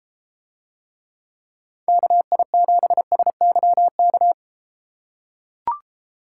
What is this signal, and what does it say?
Morse code, heard as a keyed tone, about 33 wpm, 700 Hz. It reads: KI7SYK